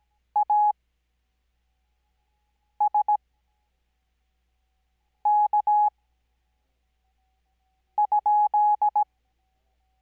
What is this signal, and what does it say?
Morse code, heard as a keyed tone, about 17 wpm, spaced as for 4 wpm, 840 Hz. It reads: ASK?
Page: Farnsworth Spacing: 15WPM Character speed Spaced at 5WPM.